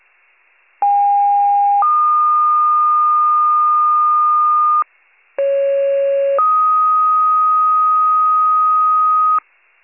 Two-tone page, heard as a one-second tone, then a three-second tone, about 810 then 1220 Hz, about 550 then 1250 Hz. It is heard from 0.8 to 9.4 s.